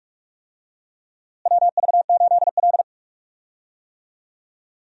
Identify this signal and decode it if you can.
Morse code, heard as a keyed tone, about 43 wpm, 700 Hz. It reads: WV8L
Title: USA Call Signs - 4 Characters - Repeated 2x - 45wpm